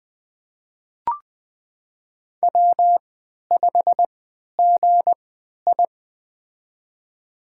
Morse code, heard as a keyed tone, about 20 wpm, spaced as for 11 wpm, 700 Hz. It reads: W5GI
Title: USA Call Signs - 4 Characters - 3x Character Spacing - 20wpm